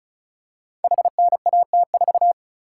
Morse code, heard as a keyed tone, about 36 wpm, 700 Hz. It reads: HNAT4
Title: Code Groups - 5 Letter:Numbers - 35wpm